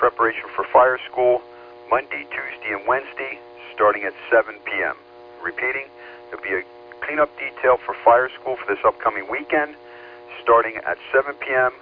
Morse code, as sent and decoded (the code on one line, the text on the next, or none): none